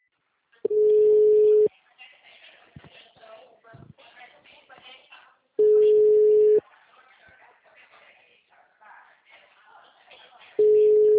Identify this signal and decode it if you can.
Ringback tone (Europe, 425 Hz), with speech.